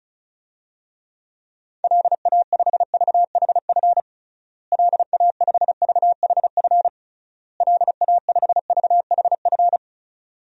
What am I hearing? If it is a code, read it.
Morse code, heard as a keyed tone, about 36 wpm, 700 Hz. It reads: LA5VHF LA5VHF LA5VHF
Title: DXCC-100 Call Signs - 6 Characters - Repeated 3x - 35wpm